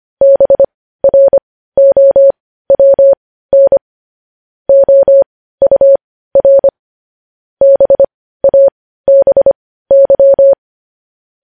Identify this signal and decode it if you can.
Morse code, heard as a keyed tone, about 25 wpm, spaced as for 15 wpm, 550 Hz. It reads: BROWN OUR BABY
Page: news-headlines-in-morse-code-at-15-wpm